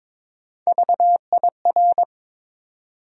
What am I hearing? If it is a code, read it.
Morse code, heard as a keyed tone, about 22 wpm, 700 Hz. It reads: VIR
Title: USA Call Sign Postfixes - 3 Letters 22wpm